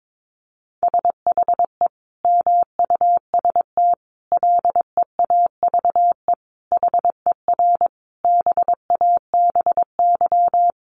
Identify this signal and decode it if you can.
Morse code, heard as a keyed tone, about 22 wpm, 700 Hz. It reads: SHE MUST LEAVE HER BABY
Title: Sentences from Top 1300 Words 22wpm